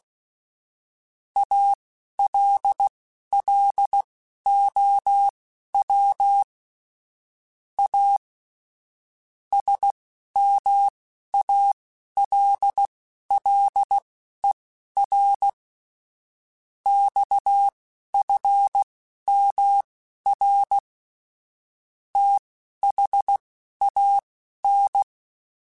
Morse code, heard as a keyed tone, about 16 wpm, spaced as for 11 wpm, 800 Hz. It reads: ALLOW A SMALLER XFMR THAN
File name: Technical Subjects / Technical Subjects-mp3s / Sw Pwr Sup @10